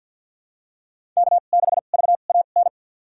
Morse code, heard as a keyed tone, about 48 wpm, spaced as for 35 wpm, 700 Hz. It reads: K6VAN
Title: USA Call Signs - 5 Characters - 2x Character Spacing - 50wpm